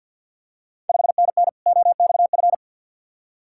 Morse code, heard as a keyed tone, about 46 wpm, 700 Hz. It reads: 5NN OXF
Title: Ontario QSO Party Exchange 50wpm